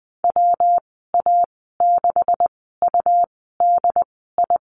Morse code, heard as a keyed tone, about 20 wpm, spaced as for 14 wpm, 700 Hz. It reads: WA6UDI
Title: 193 Lesson 31g - Letter v - Call Signs 20wpm